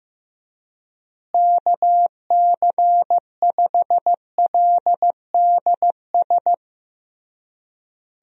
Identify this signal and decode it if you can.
Morse code, heard as a keyed tone, about 15 wpm, 700 Hz. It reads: KC5LDS
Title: USA Call Signs - 6 Characters - Repeated 3x - 15wpm